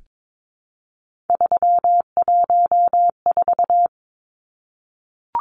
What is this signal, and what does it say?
Morse code, heard as a keyed tone, about 22 wpm, 700 Hz. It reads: 314
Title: Numbers - 3-Digit 22wpm